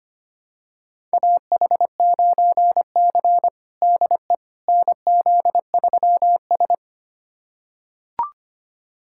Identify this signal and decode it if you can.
Morse code, heard as a keyed tone, about 25 wpm, 700 Hz. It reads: AH9C DE NZ3S
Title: USA Callsign DE Callsign - 4 Characters - 25wpm